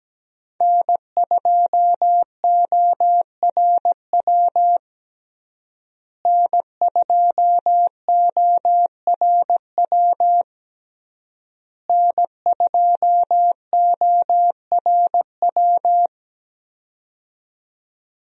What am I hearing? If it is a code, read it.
Morse code, heard as a keyed tone, about 17 wpm, 700 Hz. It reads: N2ORW N2ORW N2ORW
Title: USA Call Signs - 5 Characters - Repeated 3x - 17wpm